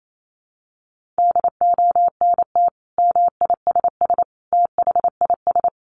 Morse code, heard as a keyed tone, about 28 wpm, 700 Hz. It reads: DONT MISS THIS
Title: Sentences from Top 800 Words 28wpm